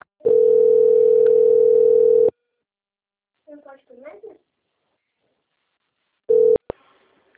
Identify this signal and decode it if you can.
Ringback tone (North America), 440 + 480 Hz, with speech.
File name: out- - - - - .7683